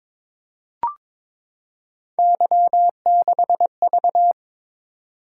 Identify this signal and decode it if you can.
Morse code, heard as a keyed tone, about 22 wpm, 700 Hz. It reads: Y6V
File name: practice / 22wpm / Three Letter-Number Combinations - 22wpm